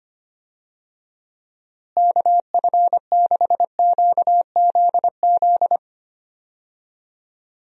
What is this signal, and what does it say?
Morse code, heard as a keyed tone, about 25 wpm, 700 Hz. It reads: KF6QZZ
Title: USA Call Signs - 6 Characters - Repeated 2x - 25wpm